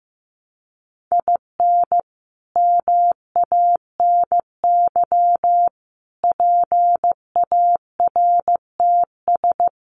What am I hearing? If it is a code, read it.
Morse code, heard as a keyed tone, about 15 wpm, 700 Hz. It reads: IN MANY PARTS
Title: Sets of 5 Words Encore 15wpm